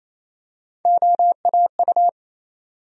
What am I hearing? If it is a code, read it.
Morse code, heard as a keyed tone, about 27 wpm, 700 Hz. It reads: OAU
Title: USA Call Sign Postfixes - 3 Letters 28wpm